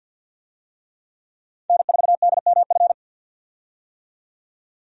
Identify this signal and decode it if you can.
Morse code, heard as a keyed tone, about 50 wpm, 700 Hz. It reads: N4DMF
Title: USA Call Signs - 5 Characters - 50wpm